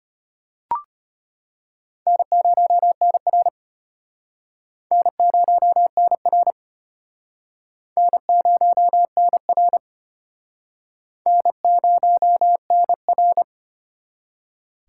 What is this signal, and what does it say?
Morse code, heard as a keyed tone, about 33 wpm, 700 Hz. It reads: N0NR N0NR N0NR N0NR